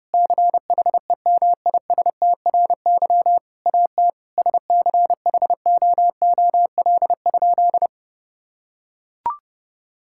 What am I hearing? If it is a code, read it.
Morse code, heard as a keyed tone, about 30 wpm, 700 Hz. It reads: CHEMISTRY AT SCHOOL?
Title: Sentences from Top 1000 Words - Review Part 6 - 30wpm